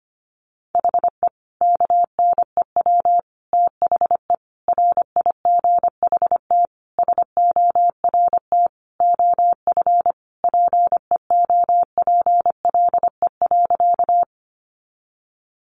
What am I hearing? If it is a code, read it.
Morse code, heard as a keyed tone, about 25 wpm, 700 Hz. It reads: HE KNEW THE RIGHT SORT OF PEOPLE.
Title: Sentences from Top 1300 Words 25wpm